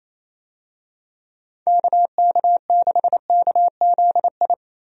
Morse code, heard as a keyed tone, about 27 wpm, 700 Hz. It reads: KK6KZI
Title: USA Call Signs - 6 Characters - 28wpm